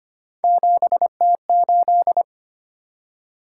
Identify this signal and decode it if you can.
Morse code, heard as a keyed tone, about 25 wpm, 700 Hz. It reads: 7T8
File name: Three Letter-Number Combinations - 25wpm